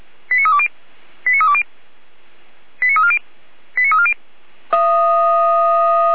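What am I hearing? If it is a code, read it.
Five-tone selective call (ZVEI), heard as quick runs of five tones, tones of about 70 ms. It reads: 89320 89320 89340 89340
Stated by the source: ZVEI five-tone selective call received on an SDR.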